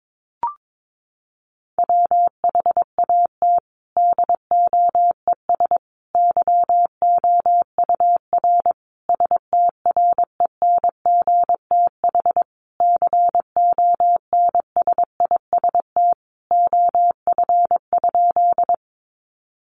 Morse code, heard as a keyed tone, about 22 wpm, 700 Hz. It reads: WHAT DOES YOUR STRENGTH CONSIST OF?
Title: Sentences from Top 1400 Words 22wpm